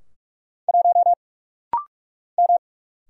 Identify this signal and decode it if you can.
Morse code, heard as a keyed tone, about 44 wpm, 700 Hz. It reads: 1 M
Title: Single Letter-Number Alt - Rapid-Fire 45wpm